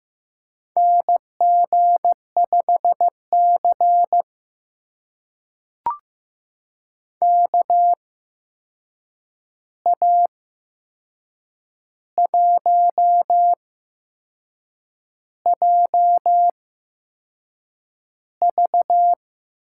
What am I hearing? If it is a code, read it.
Morse code, heard as a keyed tone, about 15 wpm, spaced as for 6 wpm, 700 Hz. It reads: NG5C KA1JV
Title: USA Call Signs - 8x Character Spacing - 15wpm